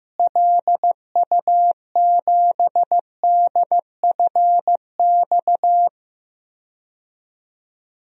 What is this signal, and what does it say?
Morse code, heard as a keyed tone, about 15 wpm, 700 Hz. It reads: LU7DFX
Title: DXCC-100 Call Signs - 6 Characters - Repeated 2x - 15wpm